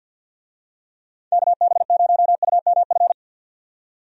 Morse code, heard as a keyed tone, about 53 wpm, 700 Hz. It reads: KB0UMF
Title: USA Call Signs - 6 Characters - 50wpm